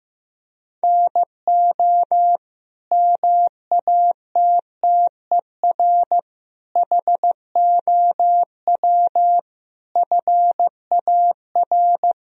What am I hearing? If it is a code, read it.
Morse code, heard as a keyed tone, about 15 wpm, 700 Hz. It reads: NO MATTER HOW FAR
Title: Sentences from Top 1000 Words - Review Part 1 - 15wpm